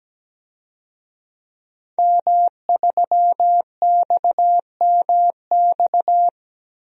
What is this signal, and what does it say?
Morse code, heard as a keyed tone, about 17 wpm, 700 Hz. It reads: M3XMX
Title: DXCC-100 Call Signs - 5 Characters - 17wpm